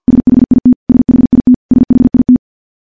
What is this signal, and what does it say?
Morse code, heard as a keyed tone, about 50 wpm, 250 Hz. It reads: SHIT SHIT SHIT
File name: morse / morse-training-50WPM